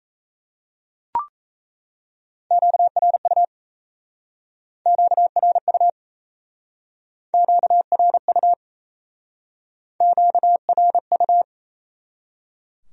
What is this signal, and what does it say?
Morse code, heard as a keyed tone, about 36 wpm, 700 Hz. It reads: QRU QRU QRU QRU